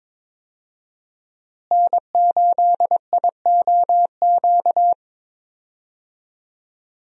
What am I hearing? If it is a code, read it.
Morse code, heard as a keyed tone, about 22 wpm, 700 Hz. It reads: N8IOQ
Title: USA Call Signs - 5 Characters - Repeated 2x - 22wpm